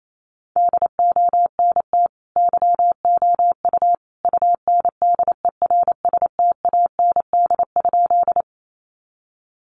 Morse code, heard as a keyed tone, about 28 wpm, 700 Hz. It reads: DONT YOU UNDERSTAND?
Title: Sentences from Top 600 Words 28wpm